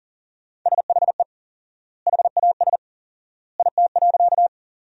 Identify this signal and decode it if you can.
Morse code, heard as a keyed tone, about 37 wpm, 700 Hz. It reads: SHE HAS IT.
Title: 4x Word-Spacing - Sentences from Top 100 Words - 40wpm